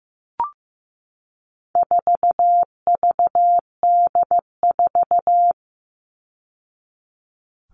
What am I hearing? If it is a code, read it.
Morse code, heard as a keyed tone, about 15 wpm, 700 Hz. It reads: 4VD4
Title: All Cut Numbers - 4-Digit 15wpm